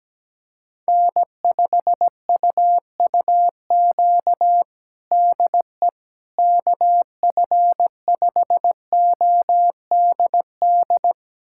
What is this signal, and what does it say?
Morse code, heard as a keyed tone, about 17 wpm, 700 Hz. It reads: N5UUQ DE KF5ODD
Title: USA Callsign DE Callsign 17wpm